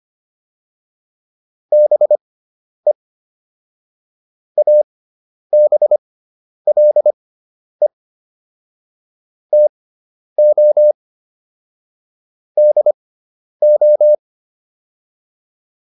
Morse code, heard as a keyed tone, about 25 wpm, spaced as for 10 wpm, 600 Hz. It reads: BE ABLE TO DO